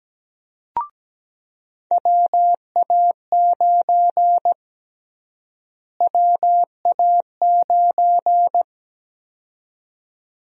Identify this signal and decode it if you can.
Morse code, heard as a keyed tone, about 17 wpm, 700 Hz. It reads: WA9 WA9
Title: USA Call Sign Prefixes - Repeated 2x - 17wpm